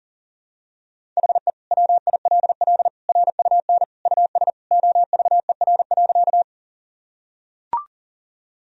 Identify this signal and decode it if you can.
Morse code, heard as a keyed tone, about 40 wpm, 700 Hz. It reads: HE WILL RUN US OVER.
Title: Sentences from Top 1000 Words - Review Part 1 - 40wpm